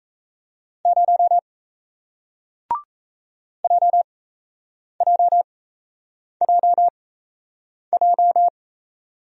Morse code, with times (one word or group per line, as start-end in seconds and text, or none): 0.9-1.4 0
3.6-8.5 JJJJ